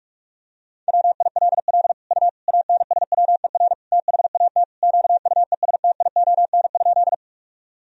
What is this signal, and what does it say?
Morse code, heard as a keyed tone, about 43 wpm, 700 Hz. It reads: WILL U ANSWER THAT QUESTION?
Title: Sentences from Top 400 Words and Abbreviations 45wpm